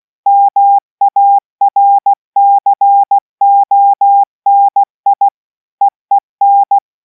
Morse code, heard as a keyed tone, about 16 wpm, 800 Hz. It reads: MARCONI EEN